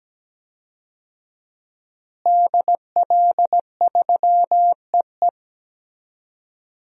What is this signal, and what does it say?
Morse code, heard as a keyed tone, about 17 wpm, 700 Hz. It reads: DL3EE
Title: DXCC-100 Call Signs - 5 Characters - Repeated 3x - 17wpm